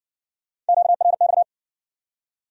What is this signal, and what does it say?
Morse code, heard as a keyed tone, about 56 wpm, 700 Hz. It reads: QRX